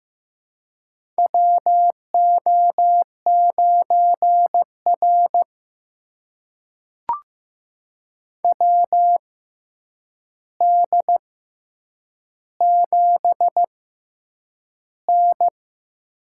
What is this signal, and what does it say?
Morse code, heard as a keyed tone, about 15 wpm, spaced as for 8 wpm, 700 Hz. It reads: WO9R WD7N
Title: USA Call Signs - 4 Characters - 6x Character Spacing - 15wpm